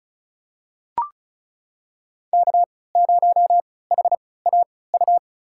Morse code, heard as a keyed tone, about 36 wpm, spaced as for 20 wpm, 690 Hz. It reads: K0HAU
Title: USA Call Signs - 3x Character Spacing - 35wpm